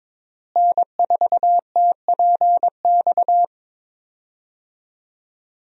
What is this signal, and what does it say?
Morse code, heard as a keyed tone, about 22 wpm, 700 Hz. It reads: N4TPX
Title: USA Call Signs - 5 Characters - Repeated 3x - 22wpm